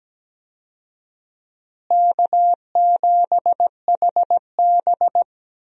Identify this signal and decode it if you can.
Morse code, heard as a keyed tone, about 17 wpm, 700 Hz. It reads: K7HB
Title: USA Call Signs - 4 Characters - Repeated 3x - 17wpm